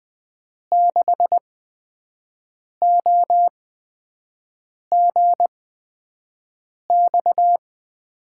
Morse code, heard as a keyed tone, about 20 wpm, spaced as for 5 wpm, 700 Hz. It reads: 6OGX
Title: USA Call Signs - 6 Characters - 8x Character Spacing - 20wpm